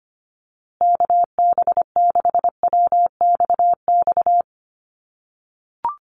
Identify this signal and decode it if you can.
Morse code, heard as a keyed tone, about 25 wpm, 700 Hz. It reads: KB6WXX
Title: USA Call Signs Repeated 2x 25wpm